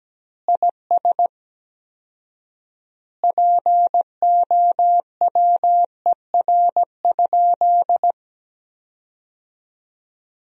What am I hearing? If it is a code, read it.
Morse code, heard as a keyed tone, about 17 wpm, 700 Hz. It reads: IS POWER?
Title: 4x Word-Spacing - Sentences from Top 500 Words - 17wpm